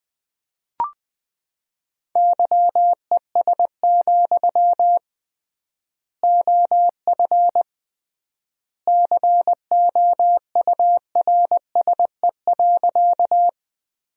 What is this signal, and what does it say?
Morse code, heard as a keyed tone, about 21 wpm, 700 Hz. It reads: YES, OF COURSE.
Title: 3x Word-Spacing - Sentences from Top 500 Words - 20wpm